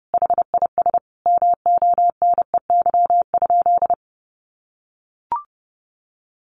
Morse code, heard as a keyed tone, about 30 wpm, 700 Hz. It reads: HIS MONEY?
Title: Sentences from Top 400 Words 30wpm